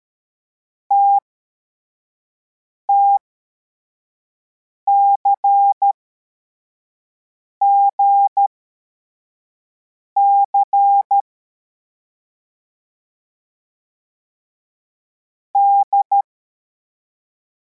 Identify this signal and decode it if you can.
Morse code, heard as a keyed tone, about 13 wpm, spaced as for 4 wpm, 800 Hz. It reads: TTCGC D